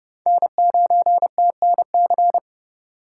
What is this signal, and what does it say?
Morse code, heard as a keyed tone, about 30 wpm, 700 Hz. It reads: N9TNC